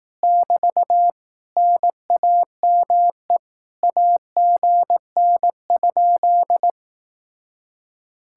Morse code, heard as a keyed tone, about 18 wpm, 700 Hz. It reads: <BT> NAME AGN?
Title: Speed-Racing Common QSO Phrases Encore - 15wpm